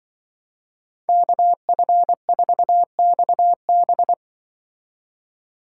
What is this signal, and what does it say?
Morse code, heard as a keyed tone, about 24 wpm, 700 Hz. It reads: KF4XB
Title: Speed Racing - USA 2x2 Call Signs 20wpm